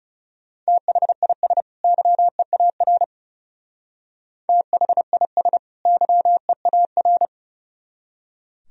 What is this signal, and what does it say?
Morse code, heard as a keyed tone, about 37 wpm, 700 Hz. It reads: THIS YEAR THIS YEAR
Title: Speed-Racing - Sets of 2 Words 30wpm